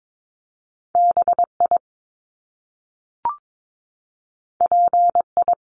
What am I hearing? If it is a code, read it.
Morse code, heard as a keyed tone, about 22 wpm, 700 Hz. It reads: BI PI
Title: Top Two Letter Combinations in Words 22wpm